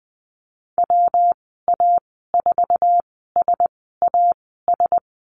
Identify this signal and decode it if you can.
Morse code, heard as a keyed tone, about 20 wpm, spaced as for 14 wpm, 700 Hz. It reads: WA4SAS
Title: 193 Lesson 31g - Letter v - Call Signs 20wpm